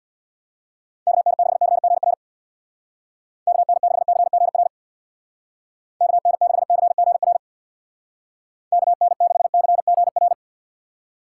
Morse code, heard as a keyed tone, about 64 wpm, 700 Hz. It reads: KN6CGR KN6CGR KN6CGR KN6CGR